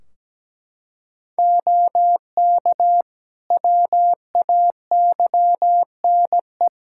Morse code, heard as a keyed tone, about 17 wpm, 700 Hz. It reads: OK WAYNE
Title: Practice QSO - Exchange 3 - More Info - 17wpm